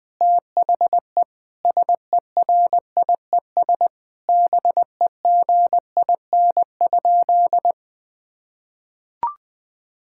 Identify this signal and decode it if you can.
Morse code, heard as a keyed tone, about 21 wpm, 700 Hz. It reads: THE SERIES BEGIN?
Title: Sentences from Top 1000 Words - Review Part 5 - 20wpm